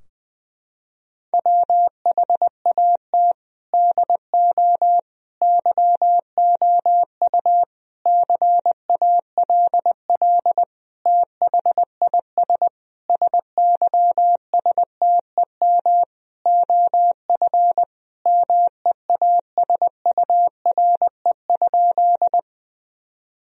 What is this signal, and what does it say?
Morse code, heard as a keyed tone, about 20 wpm, 700 Hz. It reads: WHAT DO YOU CALL THIS SYSTEM OF MEASURE?
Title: Sentences from Top 1000 Words - Review Part 3 - 20wpm